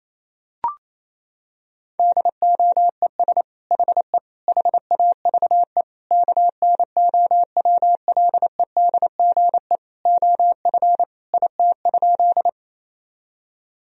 Morse code, heard as a keyed tone, about 28 wpm, 700 Hz. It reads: DOES HE HAVE KNOWLEDGE OF IT?
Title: Sentences from Top 1000 Words - Review Part 3 - 28wpm